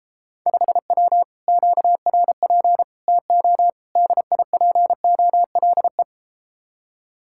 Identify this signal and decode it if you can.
Morse code, heard as a keyed tone, about 33 wpm, 700 Hz. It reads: 5W QRP TO DIPOLE